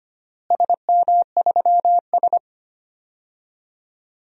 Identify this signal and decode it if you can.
Morse code, heard as a keyed tone, about 25 wpm, 700 Hz. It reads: SM3S
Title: DXCC-100 Call Signs - 4 Characters - Repeated 3x - 25wpm